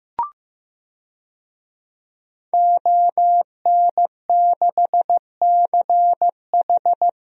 Morse code, heard as a keyed tone, about 16 wpm, 700 Hz. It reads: ON6CH
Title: DXCC-100 Call Signs - 5 Characters - Repeated 3x - 15wpm